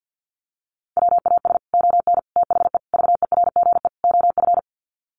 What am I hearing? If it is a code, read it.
Morse code, heard as a keyed tone, about 50 wpm, 700 Hz. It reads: WAS ON THE VERGE OF
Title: Sets of 5 Words Encore 50wpm